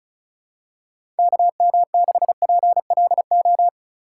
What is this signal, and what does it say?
Morse code, heard as a keyed tone, about 34 wpm, 700 Hz. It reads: KM6PLO
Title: USA Call Signs - 3x Character Spacing - 35wpm